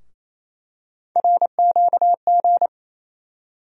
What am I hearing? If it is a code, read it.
Morse code, heard as a keyed tone, about 28 wpm, 700 Hz. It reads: RQG
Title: USA Call Sign Postfixes - 3 Letters 28wpm